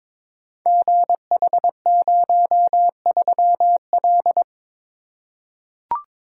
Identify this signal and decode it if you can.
Morse code, heard as a keyed tone, about 22 wpm, 700 Hz. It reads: GH03L
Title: Code Groups - 5 Letter:Numbers - 22wpm